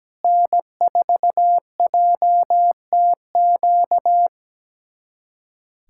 Morse code, heard as a keyed tone, about 17 wpm, 700 Hz. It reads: N4JTQ
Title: USA Callsign DE Callsign - 5 Characters - 17wpm